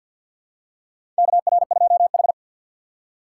Morse code, heard as a keyed tone, about 53 wpm, 700 Hz. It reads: KR2H
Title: USA Call Signs - 4x Character Spacing - 50wpm